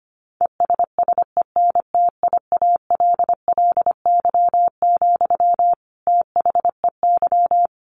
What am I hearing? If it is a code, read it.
Morse code, heard as a keyed tone, about 25 wpm, 700 Hz. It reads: ESSENTIALLY, THEY
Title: Sentences from Top 1300 Words 25wpm